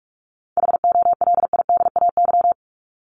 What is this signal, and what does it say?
Morse code, heard as a keyed tone, about 44 wpm, 700 Hz. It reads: HOLIDAY
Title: Top 1501-1750 Words 45wpm